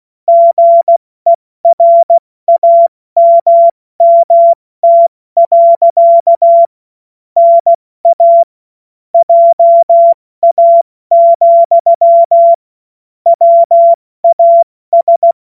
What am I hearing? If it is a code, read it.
Morse code, heard as a keyed tone, about 16 wpm, 680 Hz. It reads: GERAMMT. NA JA, WAS
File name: Morsekurs / Witze / Witz 02 April 2024 70 F80